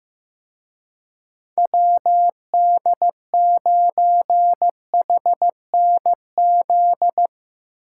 Morse code, heard as a keyed tone, about 17 wpm, 700 Hz. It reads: WD9HNZ
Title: Speed Racing - USA 2x3 Call Signs 15wpm